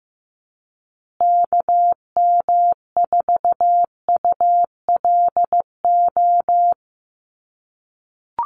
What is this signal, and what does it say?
Morse code, heard as a keyed tone, about 15 wpm, 700 Hz. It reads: KM4ULO
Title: USA Call Signs Repeated 3x 15wpm